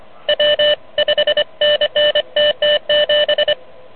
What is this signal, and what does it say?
Morse code, heard as a keyed tone, about 29 wpm, 590 Hz, with noise. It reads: W 5NN TT7
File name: truSDX / MWC29.8